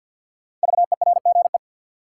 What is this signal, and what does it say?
Morse code, heard as a keyed tone, about 53 wpm, 700 Hz. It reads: VERGE
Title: Sets of 4 Words - New Word Drill - 50wpm